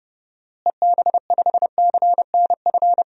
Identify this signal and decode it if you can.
Morse code, heard as a keyed tone, about 30 wpm, 700 Hz. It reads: EB5CNF
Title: DXCC-100 Call Signs - 6 Characters - Repeated 2x - 30wpm